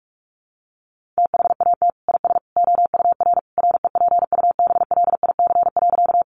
Morse code, heard as a keyed tone, about 44 wpm, 700 Hz. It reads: THAT IS OUR REPUBLIC.
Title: Sentences from Top 1500 Words 45wpm